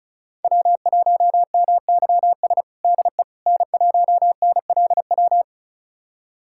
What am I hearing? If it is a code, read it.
Morse code, heard as a keyed tone, about 34 wpm, 700 Hz. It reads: W1MYS DE N1NLW
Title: USA Callsign DE Callsign - 5 Characters - 35wpm